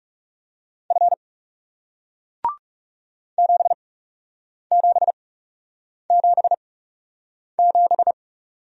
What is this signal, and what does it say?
Morse code, heard as a keyed tone, about 40 wpm, spaced as for 8 wpm, 700 Hz. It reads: F 7777